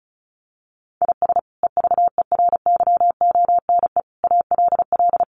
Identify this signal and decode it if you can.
Morse code, heard as a keyed tone, about 34 wpm, 700 Hz. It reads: IS EVERYONE ALL
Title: Sentences from Top 800 Words 35wpm